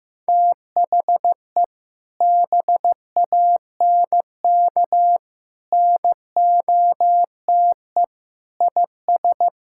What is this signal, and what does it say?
Morse code, heard as a keyed tone, about 15 wpm, 700 Hz. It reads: THE BANK NOTE IS
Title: Sentences from Top 1000 Words - Review Part 6 - 15wpm